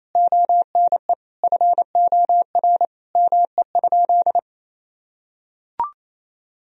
Morse code, heard as a keyed tone, about 27 wpm, 700 Hz. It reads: ONE FOR ME?
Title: Sentences from Top 1000 Words - Review Part 3 - 28wpm